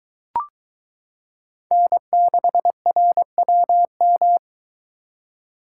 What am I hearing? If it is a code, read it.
Morse code, heard as a keyed tone, about 21 wpm, 700 Hz. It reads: N6RWM